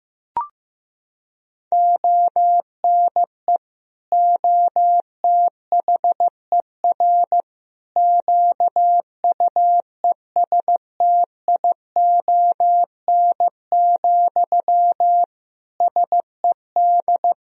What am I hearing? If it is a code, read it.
Morse code, heard as a keyed tone, about 16 wpm, 700 Hz. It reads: ONE OTHER QUESTION, SED
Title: Sentences from Top 400 Words and Abbreviations 15wpm